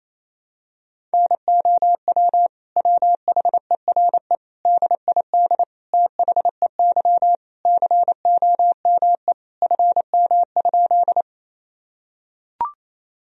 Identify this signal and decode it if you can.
Morse code, heard as a keyed tone, about 28 wpm, 700 Hz. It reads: NOW WHERE DID THEY COME FM?
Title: Sentences from Top 200 Words and Abbreviations 28wpm